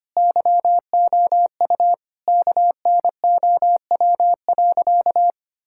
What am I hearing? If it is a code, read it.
Morse code, heard as a keyed tone, about 25 wpm, 700 Hz. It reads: YOU KNOW.
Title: Sentences from Top 1000 Words - Review Part 5 - 25wpm